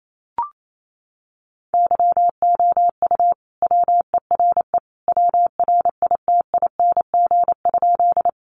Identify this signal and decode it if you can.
Morse code, heard as a keyed tone, about 28 wpm, 700 Hz. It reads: YOU WERE WRITING?
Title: Sentences from Top 800 Words 28wpm